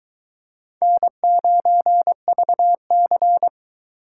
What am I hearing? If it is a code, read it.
Morse code, heard as a keyed tone, about 22 wpm, 700 Hz. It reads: N9VC